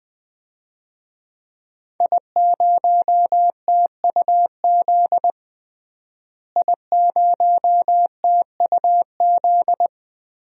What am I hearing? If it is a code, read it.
Morse code, heard as a keyed tone, about 21 wpm, 700 Hz. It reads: I0TUZ I0TUZ
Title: DXCC-100 Call Signs - 5 Characters - Repeated 3x - 20wpm